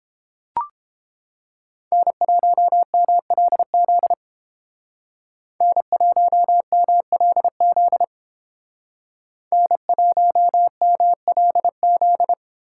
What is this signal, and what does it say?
Morse code, heard as a keyed tone, about 29 wpm, 700 Hz. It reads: N1MLZ N1MLZ N1MLZ